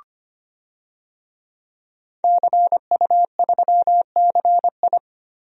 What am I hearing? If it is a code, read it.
Morse code, heard as a keyed tone, about 25 wpm, 700 Hz. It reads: CU3CI
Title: DXCC-100 Call Signs - 5 Characters - Repeated 2x - 25wpm